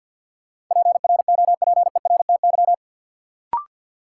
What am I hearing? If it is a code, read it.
Morse code, heard as a keyed tone, about 53 wpm, 700 Hz. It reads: PROPERTY
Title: Sentences from Top 1000 Words - New Word Drill - 50wpm